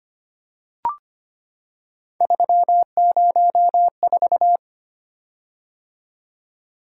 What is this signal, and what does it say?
Morse code, heard as a keyed tone, about 25 wpm, 700 Hz. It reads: 304
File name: Three Letter-Number Combinations - 25wpm